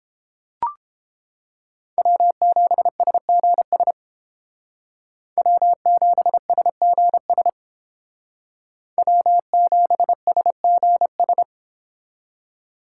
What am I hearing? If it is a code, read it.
Morse code, heard as a keyed tone, about 29 wpm, 700 Hz. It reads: W7SGS W7SGS W7SGS